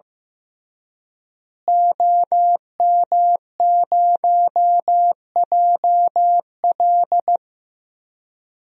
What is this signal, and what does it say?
Morse code, heard as a keyed tone, about 15 wpm, 700 Hz. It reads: OM0JL
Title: DXCC-100 Call Signs - 5 Characters - Repeated 3x - 15wpm